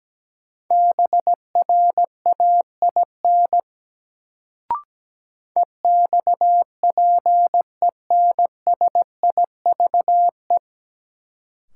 Morse code, heard as a keyed tone, about 17 wpm, 700 Hz. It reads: BRAIN EXPENSIVE
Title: Sentences from Top 1000 Words - Review Part 6 - New Word Drill - 17wpm